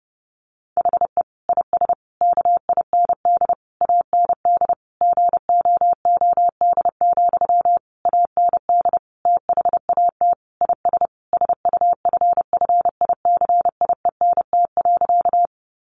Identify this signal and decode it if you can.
Morse code, heard as a keyed tone, about 28 wpm, 700 Hz. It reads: HE IS KIND AND GOOD, AND THAT IS SUFFICIENT.
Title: Sentences from Top 1400 Words 30wpm